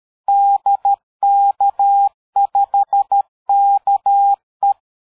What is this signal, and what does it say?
Morse code, heard as a keyed tone, about 13 wpm, 800 Hz. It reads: DK5KE
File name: dk5ke / QSO 60BpM